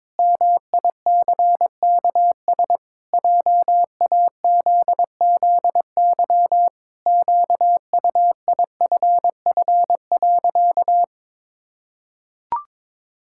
Morse code, heard as a keyed tone, about 22 wpm, 700 Hz. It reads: MICKS JAZZY QUIFF.
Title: Pangram Challenge v2 - 22wpm